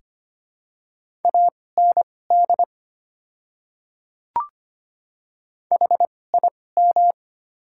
Morse code, heard as a keyed tone, about 25 wpm, spaced as for 18 wpm, 690 Hz. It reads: AND HIM
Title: 2x Character Spacing - Top 200 Words - 25wpm